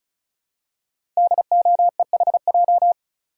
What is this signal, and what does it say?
Morse code, heard as a keyed tone, about 36 wpm, 700 Hz. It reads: DOEHJ
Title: Code Groups - 5 Letters - 35wpm